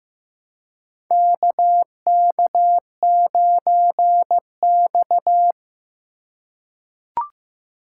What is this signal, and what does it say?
Morse code, heard as a keyed tone, about 15 wpm, 700 Hz. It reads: KK9X
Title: USA Call Signs - 2x Character Spacing - 15wpm